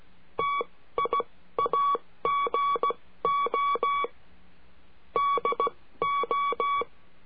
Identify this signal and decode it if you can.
Morse code, heard as a keyed tone, about 16 wpm, spaced as for 13 wpm, 1130 Hz. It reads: TIAGO DO
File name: cursos / CURSO DE TELEGRAFIA / Telegrafia - Trilha 22